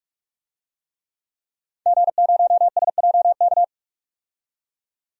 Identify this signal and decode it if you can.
Morse code, heard as a keyed tone, about 46 wpm, 700 Hz. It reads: G0SJK
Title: DXCC-100 Call Signs - 5 Characters - 45wpm